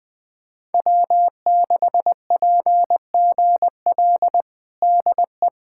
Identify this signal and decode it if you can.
Morse code, heard as a keyed tone, about 20 wpm, 700 Hz. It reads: W6PGL DE
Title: USA Callsign DE Callsign - 5 Characters - 20wpm